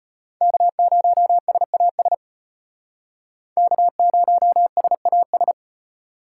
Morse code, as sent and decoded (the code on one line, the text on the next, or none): -.- ----- ... .- ... / -.- ----- ... .- ...
K0SAS K0SAS